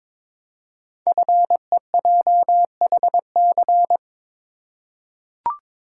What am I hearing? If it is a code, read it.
Morse code, heard as a keyed tone, about 22 wpm, 700 Hz. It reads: FEJHC
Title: Code Groups - 5 Letters - 22wpm